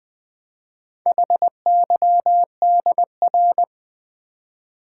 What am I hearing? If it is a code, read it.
Morse code, heard as a keyed tone, about 20 wpm, 700 Hz. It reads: HYDR
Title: Build A Word - 4 Letters At A Time - 8-Letter Words - 20wpm